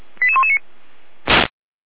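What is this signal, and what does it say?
Five-tone selective call (ZVEI), heard as quick runs of five tones, tones of about 70 ms. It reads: 88119
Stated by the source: ZVEI five-tone selective call 88057 received on SDR.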